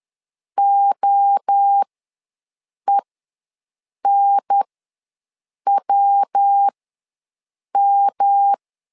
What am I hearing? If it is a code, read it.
Morse code, heard as a keyed tone, about 10 wpm, spaced as for 6 wpm, 800 Hz. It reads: OENWM
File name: LEZIONI / 3 MWRO / 3 MWRO ws 10wpm 2310